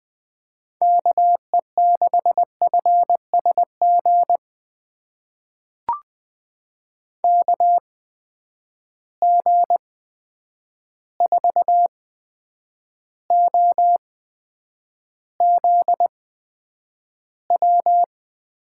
Morse code, heard as a keyed tone, about 20 wpm, spaced as for 9 wpm, 700 Hz. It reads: KE6FSG KG4OZW
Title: USA Call Signs - 6 Characters - 8x Character Spacing - 20wpm